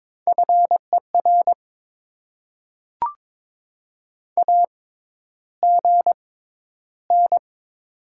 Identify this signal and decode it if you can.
Morse code, heard as a keyed tone, about 21 wpm, spaced as for 11 wpm, 700 Hz. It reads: FER AGN
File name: Top 20 QSO Elements - 6x Character Spacing - 22wpm